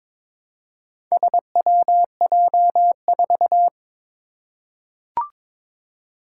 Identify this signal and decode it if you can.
Morse code, heard as a keyed tone, about 22 wpm, 700 Hz. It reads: SWJ4
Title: Four Challenging Letter-Number Combinations 22wpm